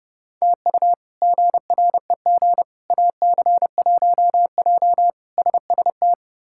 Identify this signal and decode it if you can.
Morse code, heard as a keyed tone, about 32 wpm, 700 Hz. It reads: TU GREG AC1J SST
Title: K1USN SST QSO Practice - 30wpm